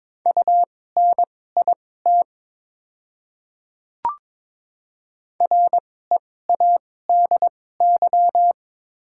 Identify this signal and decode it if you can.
Morse code, heard as a keyed tone, about 22 wpm, spaced as for 16 wpm, 700 Hz. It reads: UNIT READY
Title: 2x Character Spacing - Top 500 Words - 22wpm